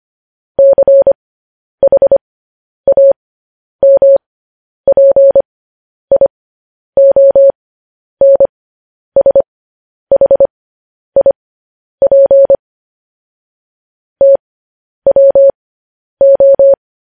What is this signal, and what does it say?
Morse code, heard as a keyed tone, about 25 wpm, spaced as for 10 wpm, 550 Hz. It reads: CHAMPIONSHIP TWO
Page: News Headlines in Morse Code at 10 WPM